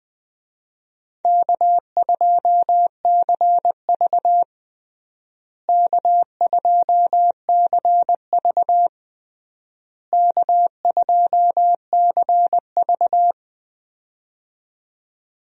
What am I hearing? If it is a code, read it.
Morse code, heard as a keyed tone, about 20 wpm, 700 Hz. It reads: K2CV K2CV K2CV
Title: USA Call Signs - 4 Characters - Repeated 3x - 20wpm